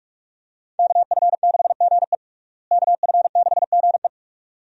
Morse code, heard as a keyed tone, about 46 wpm, 700 Hz. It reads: KF6GE KF6GE
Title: USA Call Signs - 5 Characters - Repeated 2x - 45wpm